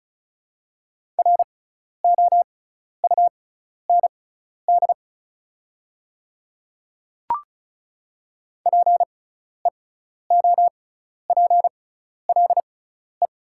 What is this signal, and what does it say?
Morse code, heard as a keyed tone, about 36 wpm, spaced as for 12 wpm, 690 Hz. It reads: ROUND PEOPLE
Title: 6x Character Spacing - Top 200 Words - 35wpm